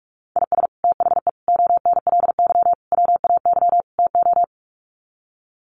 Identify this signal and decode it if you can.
Morse code, heard as a keyed tone, about 44 wpm, 700 Hz. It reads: IS THE ONLY WAY TO
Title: Sets of 5 Words Encore 45wpm